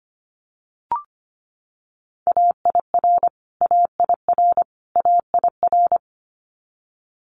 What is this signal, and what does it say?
Morse code, heard as a keyed tone, about 25 wpm, 700 Hz. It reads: AIR AIR AIR
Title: Top 500 Words Repeated 25wpm